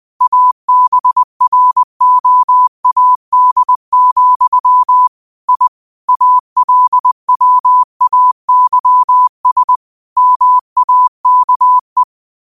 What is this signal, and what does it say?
Morse code, heard as a keyed tone, about 20 wpm, 1000 Hz. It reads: ABROAD, I ALWAYS MAKE